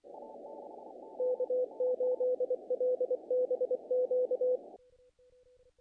Morse code, heard as a keyed tone, about 24 wpm, 500 Hz. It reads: K8LBQ
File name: contests / naqp-cw-k5tr-op-k5pi- -1800